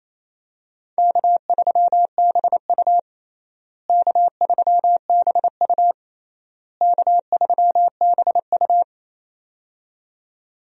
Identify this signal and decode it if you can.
Morse code, heard as a keyed tone, about 27 wpm, 700 Hz. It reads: K3BU K3BU K3BU
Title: USA Call Signs - 4 Characters - Repeated 3x - 28wpm